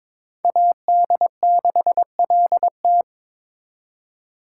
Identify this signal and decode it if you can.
Morse code, heard as a keyed tone, about 26 wpm, 700 Hz. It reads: AD6LT
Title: Speed Racing - USA 2x2 Call Signs 22wpm